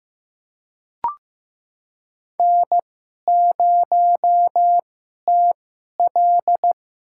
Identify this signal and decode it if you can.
Morse code, heard as a keyed tone, about 15 wpm, spaced as for 11 wpm, 700 Hz. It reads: N0TL
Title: USA Call Signs - 4 Characters - 2x Character Spacing - 15wpm